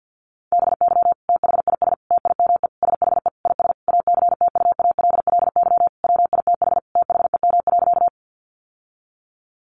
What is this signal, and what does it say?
Morse code, heard as a keyed tone, about 50 wpm, 700 Hz. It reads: BY THIS TIME SHE IS ACTUALLY WITH THEM.
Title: Sentences from Top 700 Words 50wpm